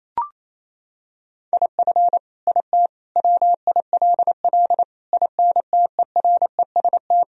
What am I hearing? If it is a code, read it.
Morse code, heard as a keyed tone, about 28 wpm, 700 Hz. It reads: IF IT WILL INTEREST
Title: Sentences from Top 1000 Words - Review Part 2 - 28wpm